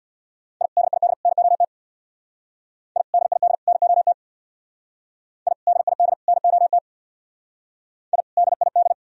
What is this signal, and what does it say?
Morse code, heard as a keyed tone, about 62 wpm, 700 Hz. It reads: I DID NOT I DID NOT I DID NOT I DID
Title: Speed-Racing - Sets of 3 Words 50wpm